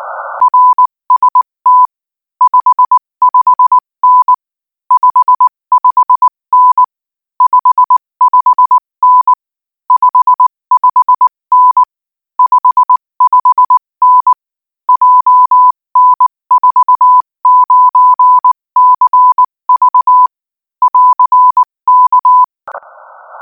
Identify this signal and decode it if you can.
Morse code, heard as a keyed tone, about 19 wpm, 1000 Hz. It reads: RST 55N 55N 55N 55N 55N JN49CV <AR>K
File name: IZ4BEH-13-cm-AS